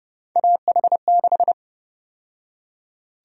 Morse code, heard as a keyed tone, about 30 wpm, 700 Hz. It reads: AH6
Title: USA Call Sign Prefixes - 30wpm